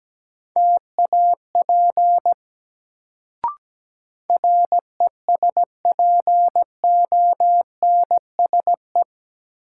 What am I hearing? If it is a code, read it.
Morse code, heard as a keyed tone, about 17 wpm, 700 Hz. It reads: TAP RESPONSE
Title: Sentences from Top 1400 Words - New Word Drill - 17wpm